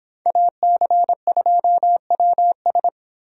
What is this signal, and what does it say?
Morse code, heard as a keyed tone, about 26 wpm, 700 Hz. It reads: AC2WS